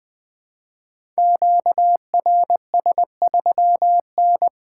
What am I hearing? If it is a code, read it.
Morse code, heard as a keyed tone, about 20 wpm, 700 Hz. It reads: QRS3N
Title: Code Groups - 5 Letter:Numbers - 20wpm